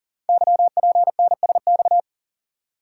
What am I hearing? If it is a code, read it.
Morse code, heard as a keyed tone, about 39 wpm, 700 Hz. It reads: YPNSX
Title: Code Groups - 5 Letter:Numbers - 40wpm